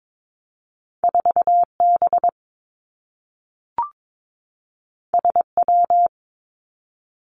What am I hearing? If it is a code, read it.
Morse code, heard as a keyed tone, about 22 wpm, 700 Hz. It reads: 4 B S W
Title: Two Challenging Letter-Number Combinations 22wpm